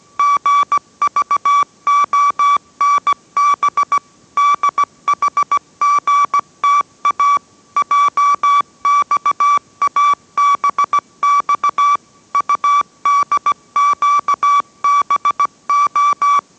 Morse code, heard as a keyed tone, about 17 wpm, 1150 Hz. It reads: GVONB DHGTA JXABX UDQBO